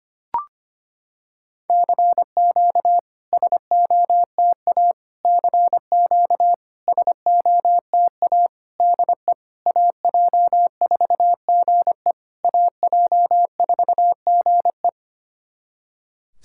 Morse code, heard as a keyed tone, about 25 wpm, 700 Hz. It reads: CQ SOTA CQ SOTA DE AJ4GE AJ4GE
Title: SOTA QSO Practice 25wpm